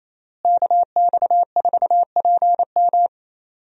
Morse code, heard as a keyed tone, about 28 wpm, 700 Hz. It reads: KX4PM
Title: USA Call Signs - 5 Characters - Repeated 3x - 28wpm